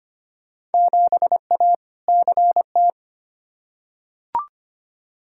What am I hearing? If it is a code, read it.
Morse code, heard as a keyed tone, about 25 wpm, 700 Hz. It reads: 7A CT
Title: Field Day Exchanges - 25wpm